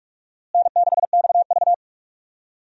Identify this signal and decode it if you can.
Morse code, heard as a keyed tone, about 46 wpm, 700 Hz. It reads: N6XV
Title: USA Call Signs - 4 Characters - Repeated 3x - 45wpm